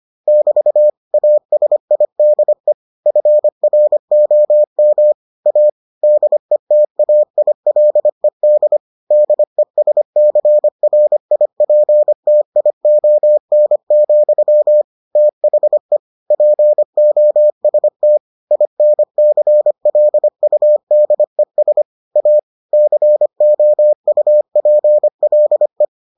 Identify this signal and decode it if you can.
Morse code, heard as a keyed tone, about 25 wpm, 600 Hz. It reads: <BT> ASIDE FROM A DETAILED DESCRIPTION, THE POST INCLUDES A COUPLE